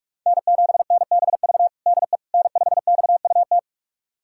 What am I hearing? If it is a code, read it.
Morse code, heard as a keyed tone, about 46 wpm, 700 Hz. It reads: N7NBV DE N5XUT
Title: USA Callsign DE Callsign - 5 Characters - 45wpm